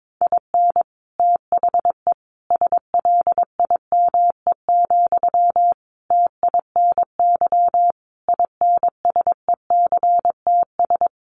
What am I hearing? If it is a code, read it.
Morse code, heard as a keyed tone, about 22 wpm, 700 Hz. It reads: IN THE SLIME, TINY INSECTS
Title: A Sound Of Thunder 22wpm